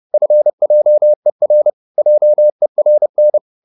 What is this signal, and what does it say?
Morse code, heard as a keyed tone, about 30 wpm, 600 Hz. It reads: FJER JERN
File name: Korte ord DK 30wpm